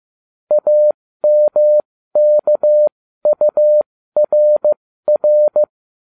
Morse code, heard as a keyed tone, about 15 wpm, spaced as for 13 wpm, 600 Hz. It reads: AMKURR
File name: cw / ubacw08 2